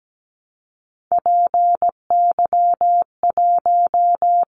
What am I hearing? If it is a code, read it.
Morse code, heard as a keyed tone, about 17 wpm, 700 Hz. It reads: PY1
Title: Callsign Prefixes for Top 50 DXCC - 17wpm